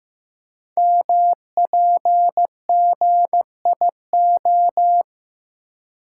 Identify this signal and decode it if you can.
Morse code, heard as a keyed tone, about 15 wpm, 700 Hz. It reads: MPGIO
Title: Code Groups - 5 Letters - 15wpm